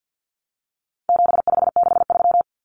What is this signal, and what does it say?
Morse code, heard as a keyed tone, about 50 wpm, 700 Hz. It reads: 7563
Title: Numbers - 4-Digit 50wpm